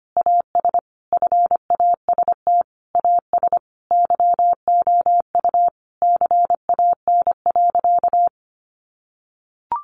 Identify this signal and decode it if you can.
Morse code, heard as a keyed tone, about 25 wpm, 700 Hz. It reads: AS FAST AS YOU CAN.
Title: Sentences from Top 600 Words 25wpm